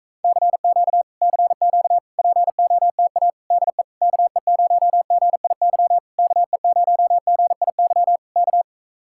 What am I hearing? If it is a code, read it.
Morse code, heard as a keyed tone, about 45 wpm, 700 Hz. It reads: CQ CQ POTA DE KE0GIY KE0GIY K